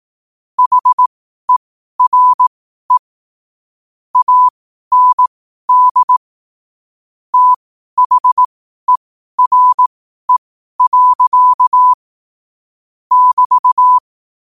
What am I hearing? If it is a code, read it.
Morse code, heard as a keyed tone, about 18 wpm, spaced as for 12 wpm, 1000 Hz. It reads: HERE AND THERE. <BT>
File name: qotdfiles / 12wpm